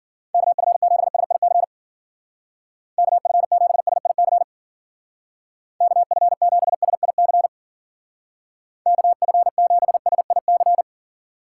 Morse code, heard as a keyed tone, about 53 wpm, 700 Hz. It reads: KF7SIC KF7SIC KF7SIC KF7SIC